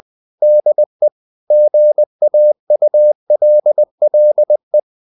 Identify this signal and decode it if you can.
Morse code, heard as a keyed tone, about 20 wpm, 600 Hz. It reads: DE GAULLE
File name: CW / Practice / EnglishQuotations0320